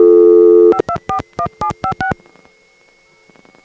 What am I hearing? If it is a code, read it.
DTMF, heard as telephone touch tones, tones of about 80 ms. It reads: 5241756